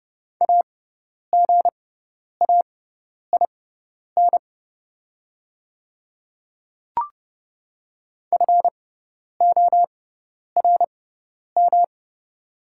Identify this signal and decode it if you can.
Morse code, heard as a keyed tone, about 30 wpm, spaced as for 10 wpm, 690 Hz. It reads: AGAIN FORM
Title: 6x Character Spacing - Top 200 Words - 30wpm